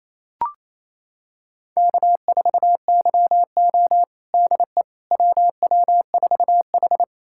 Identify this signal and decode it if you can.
Morse code, heard as a keyed tone, about 28 wpm, 700 Hz. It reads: K4YO DE WW4H
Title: USA Callsign DE Callsign - 4 Characters - 28wpm